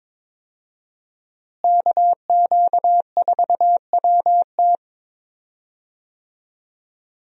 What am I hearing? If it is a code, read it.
Morse code, heard as a keyed tone, about 22 wpm, 700 Hz. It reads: KQ4WT
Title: USA Call Signs - 5 Characters - Repeated 3x - 22wpm